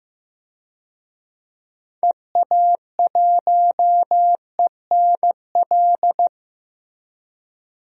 Morse code, heard as a keyed tone, about 15 wpm, 700 Hz. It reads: EA1ENL
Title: DXCC-100 Call Signs - 6 Characters - Repeated 3x - 15wpm